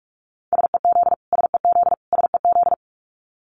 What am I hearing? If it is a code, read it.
Morse code, heard as a keyed tone, about 44 wpm, 700 Hz. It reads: SEZ SEZ SEZ
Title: Abbreviations Repeated 3x - 45wpm